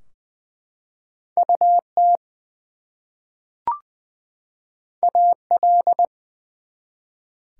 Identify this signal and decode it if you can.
Morse code, heard as a keyed tone, about 20 wpm, 700 Hz. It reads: UT AL
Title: US States and Canadian Provinces 20wpm